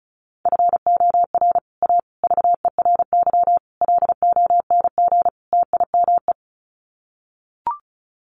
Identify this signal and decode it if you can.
Morse code, heard as a keyed tone, about 34 wpm, 700 Hz. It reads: FOR A VERY LONG TIME
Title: Sets of 5 Words 35wpm